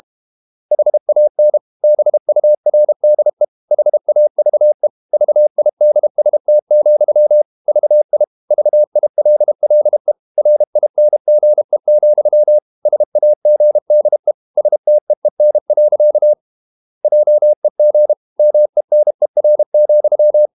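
Morse code, heard as a keyed tone, about 32 wpm, 600 Hz. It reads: HAN BURDE HAVE VIDST, VI VILLE RINGE, SAGDE STEEN. JEG MENER,
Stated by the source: High-speed QRQ CW at 32 WPM.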